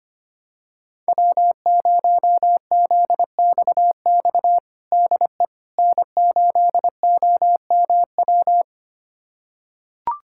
Morse code, heard as a keyed tone, about 25 wpm, 700 Hz. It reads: W0ZXX DE N8OMW
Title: USA Callsign DE Callsign - 5 Characters - 25wpm